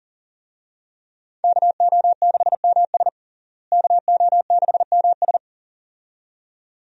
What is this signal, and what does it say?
Morse code, heard as a keyed tone, about 40 wpm, 700 Hz. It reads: KO6MS KO6MS
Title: USA Call Signs - 5 Characters - Repeated 2x - 40wpm